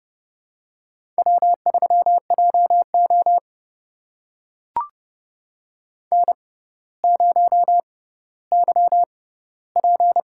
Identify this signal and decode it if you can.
Morse code, heard as a keyed tone, about 30 wpm, spaced as for 15 wpm, 700 Hz. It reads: W3JO N0YP
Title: USA Call Signs - 4 Characters - 6x Character Spacing - 30wpm